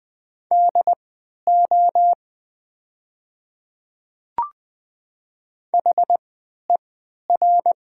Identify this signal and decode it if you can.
Morse code, heard as a keyed tone, about 20 wpm, spaced as for 11 wpm, 700 Hz. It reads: DO HER
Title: 3x Character Spacing - Top 100 Words - 20wpm